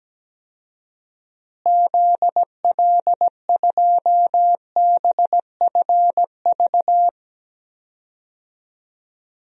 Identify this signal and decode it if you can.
Morse code, heard as a keyed tone, about 17 wpm, 700 Hz. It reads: ZL2BFV
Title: DXCC-100 Call Signs - 6 Characters - Repeated 3x - 17wpm